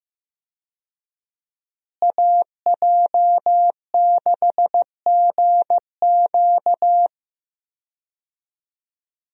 Morse code, heard as a keyed tone, about 15 wpm, 700 Hz. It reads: AJ6GQ
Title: USA Call Signs - 5 Characters - Repeated 3x - 15wpm